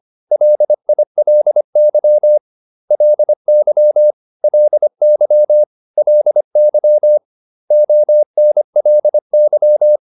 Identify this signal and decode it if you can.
Morse code, heard as a keyed tone, about 25 wpm, 600 Hz. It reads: LILY LY LY LY ONLY